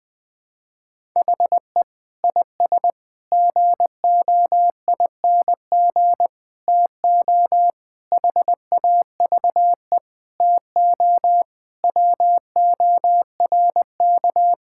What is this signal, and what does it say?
Morse code, heard as a keyed tone, about 21 wpm, 700 Hz. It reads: HE IS GOING TO HAVE TO WORK
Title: Sentences with 5-Word Phrases - 20wpm